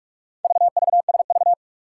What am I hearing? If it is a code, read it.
Morse code, heard as a keyed tone, about 44 wpm, 700 Hz. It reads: VVSV
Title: Four Challenging Letter-Number Combinations 45wpm